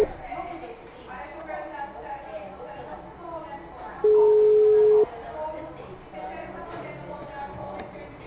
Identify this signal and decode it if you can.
Ringback tone (Europe, 425 Hz), with speech.